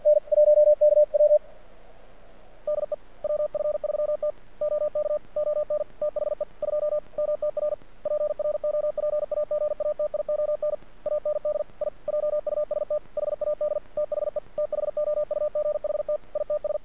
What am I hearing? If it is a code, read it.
Morse code, heard as a keyed tone, about 48 wpm, 600 Hz. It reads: N 1KW DE WF3T OK ON THE 2 MTR PROPAGATION AND I JUST HAD THE THOUGHT ITS
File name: archived / CRSnet